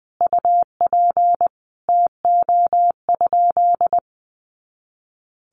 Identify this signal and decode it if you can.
Morse code, heard as a keyed tone, about 20 wpm, 700 Hz. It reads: UP TO?
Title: Sentences from Top 500 Words 20wpm